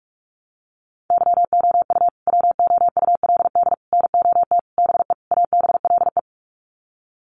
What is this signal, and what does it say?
Morse code, heard as a keyed tone, about 44 wpm, 700 Hz. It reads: YOU WOULD NOT BE ABLE
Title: Sets of 5 Words Encore2 45wpm